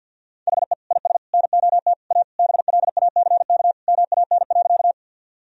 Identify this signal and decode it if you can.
Morse code, heard as a keyed tone, about 53 wpm, 700 Hz. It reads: HE IS NOT A BLACK MAN.
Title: Sentences from Top 1000 Words - Review Part 2 - 50wpm